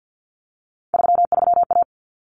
Morse code, heard as a keyed tone, about 50 wpm, 700 Hz. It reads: 33A
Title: Common Cut Numbers - 3-Digit 50wpm